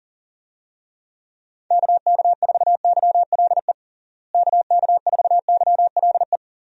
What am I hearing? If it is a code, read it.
Morse code, heard as a keyed tone, about 40 wpm, 700 Hz. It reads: KK4YLE KK4YLE
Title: USA Call Signs - 6 Characters - Repeated 2x - 40wpm